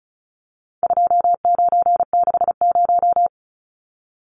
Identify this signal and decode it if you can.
Morse code, heard as a keyed tone, about 34 wpm, 700 Hz. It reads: 2960
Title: Numbers - 4-Digit 35wpm